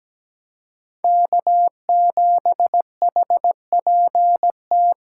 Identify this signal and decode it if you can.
Morse code, heard as a keyed tone, about 17 wpm, 700 Hz. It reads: K7HPT
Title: K1USN SST QSO Practice - 17wpm